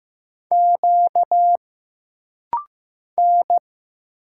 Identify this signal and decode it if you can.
Morse code, heard as a keyed tone, about 15 wpm, 700 Hz. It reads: Q N
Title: Single Letter-Number Alt - Rapid-Fire 15wpm